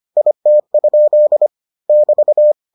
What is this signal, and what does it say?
Morse code, heard as a keyed tone, about 25 wpm, 600 Hz. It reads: IT? <BT>